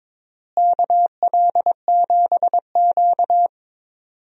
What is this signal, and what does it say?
Morse code, heard as a keyed tone, about 22 wpm, 700 Hz. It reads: KL7Q
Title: DXCC-100 Call Signs - 4 Characters - Repeated 3x - 22wpm